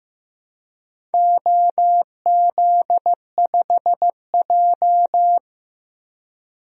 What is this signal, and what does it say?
Morse code, heard as a keyed tone, about 15 wpm, 700 Hz. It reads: OZ5J
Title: DXCC-100 Call Signs - 4 Characters - 15wpm